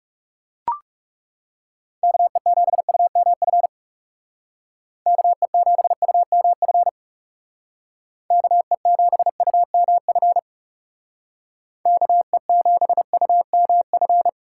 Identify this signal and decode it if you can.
Morse code, heard as a keyed tone, about 39 wpm, 700 Hz. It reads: KE7UMF KE7UMF KE7UMF KE7UMF